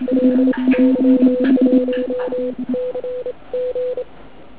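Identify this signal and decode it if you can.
Morse code, heard as a keyed tone, about 27 wpm, spaced as for 23 wpm, 500 Hz, with noise. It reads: LY1CG
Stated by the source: Morning 80m pile-up recorded at nearly the same time on three transceivers: FT1000MKV, FT817 and ATS-3B.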